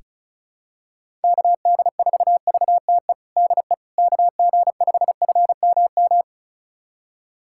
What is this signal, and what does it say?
Morse code, heard as a keyed tone, about 34 wpm, 700 Hz. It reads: KD4VTE DE KG5FMM
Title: USA Callsign DE Callsign - 6 Characters - 35wpm